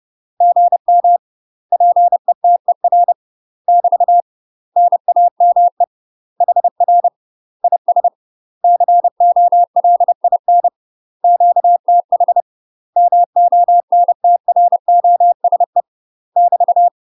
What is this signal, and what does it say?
Morse code, heard as a keyed tone, about 30 wpm, 700 Hz. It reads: GM PETER <BT> NAME HR IS COLIN QTH MONTROSE <BT>
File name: UK MASTER QSOs TRAINING / 30WPM / UK MASTER QSOs 0033